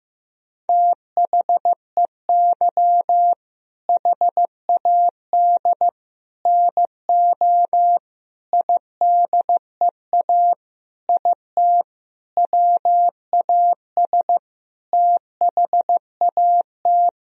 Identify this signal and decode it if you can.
Morse code, heard as a keyed tone, about 16 wpm, 700 Hz. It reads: THEY HAD NO IDEA IT WAS THAT
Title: Sentences from Top 1000 Words - Review Part 1 - 15wpm